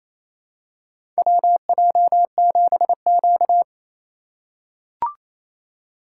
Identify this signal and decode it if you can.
Morse code, heard as a keyed tone, about 28 wpm, 700 Hz. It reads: WJ7Q
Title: USA Call Signs - 4 Characters - 4x Character Spacing - 28wpm